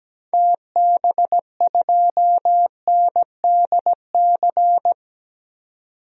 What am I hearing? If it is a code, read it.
Morse code, heard as a keyed tone, about 17 wpm, 700 Hz. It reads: TB2NDC
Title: DXCC-100 Call Signs - 6 Characters - 17wpm